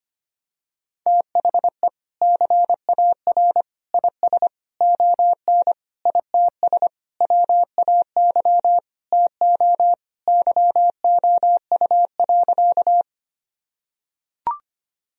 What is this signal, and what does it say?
Morse code, heard as a keyed tone, about 25 wpm, 700 Hz. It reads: THE CAR IS ON ITS WAY TO YOU.
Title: Sentences from Top 1000 Words - Review Part 1 - 25wpm